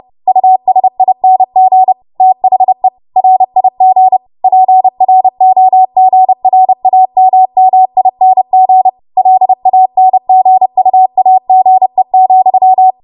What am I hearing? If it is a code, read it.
Morse code, heard as a keyed tone, about 30 wpm, 750 Hz. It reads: USING THE RIG PROGRAMMING LANGUAGE,
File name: Morse / 30WPM